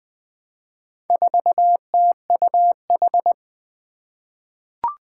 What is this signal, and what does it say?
Morse code, heard as a keyed tone, about 20 wpm, 700 Hz. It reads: 4TUH
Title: Four Challenging Letter-Number Combinations 20wpm